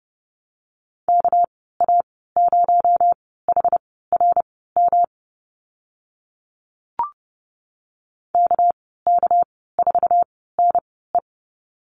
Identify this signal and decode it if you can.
Morse code, heard as a keyed tone, about 30 wpm, spaced as for 17 wpm, 700 Hz. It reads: KA0HRM KK4NE
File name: Lesson 37g - Letter x - Call Signs 30wpm